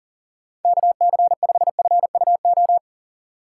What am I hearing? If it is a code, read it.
Morse code, heard as a keyed tone, about 40 wpm, 700 Hz. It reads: KC5FUO
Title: USA Call Signs - 6 Characters - 6x Character Spacing - 40wpm